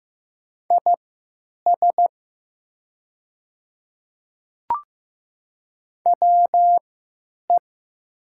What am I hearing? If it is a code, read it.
Morse code, heard as a keyed tone, about 15 wpm, spaced as for 8 wpm, 700 Hz. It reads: IS WE
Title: 3x Character Spacing - Top 100 Words - 15wpm